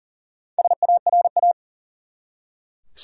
Morse code, heard as a keyed tone, about 40 wpm, 700 Hz. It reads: SARA
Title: USA Common Names - Max 4 Letters - 40wpm